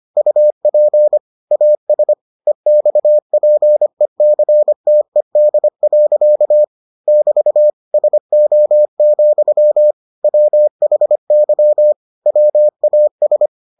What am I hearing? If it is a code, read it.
Morse code, heard as a keyed tone, about 25 wpm, 600 Hz. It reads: UP AS EXPECTED. <BT> SO, WHY WAS